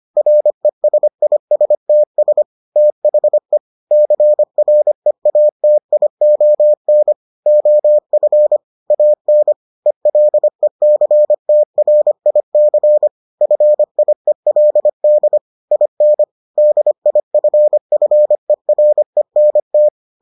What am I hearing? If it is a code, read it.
Morse code, heard as a keyed tone, about 25 wpm, 600 Hz. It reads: RESISTS THE CREATION OF AN ELECTRIC FIELD IN DIFFERENT